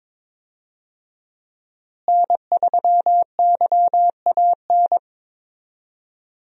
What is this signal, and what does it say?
Morse code, heard as a keyed tone, about 22 wpm, 700 Hz. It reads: N3YAN
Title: USA Call Signs - 5 Characters - Repeated 3x - 22wpm